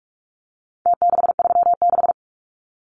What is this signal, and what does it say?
Morse code, heard as a keyed tone, about 44 wpm, 700 Hz. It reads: T636
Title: Numbers - Contest Style - 4-Digit 45wpm